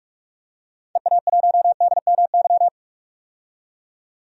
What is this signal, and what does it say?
Morse code, heard as a keyed tone, about 44 wpm, 700 Hz. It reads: EA1DMY
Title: DXCC-100 Call Signs - 6 Characters - 45wpm